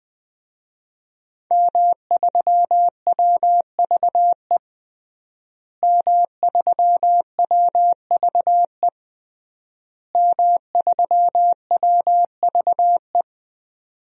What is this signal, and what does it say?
Morse code, heard as a keyed tone, about 20 wpm, 700 Hz. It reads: M3WVE M3WVE M3WVE
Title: DXCC-100 Call Signs - 5 Characters - Repeated 3x - 20wpm